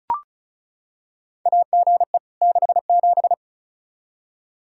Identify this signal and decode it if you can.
Morse code, heard as a keyed tone, about 36 wpm, 700 Hz. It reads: AGE 67
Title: K1USN SST QSO Practice 35wpm